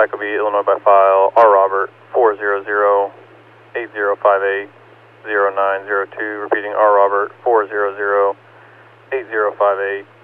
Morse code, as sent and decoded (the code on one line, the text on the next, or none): none